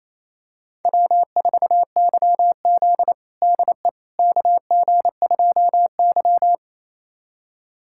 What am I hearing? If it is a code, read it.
Morse code, heard as a keyed tone, about 28 wpm, 700 Hz. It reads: W4YZ DE KG2Y
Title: USA Callsign DE Callsign - 4 Characters - 28wpm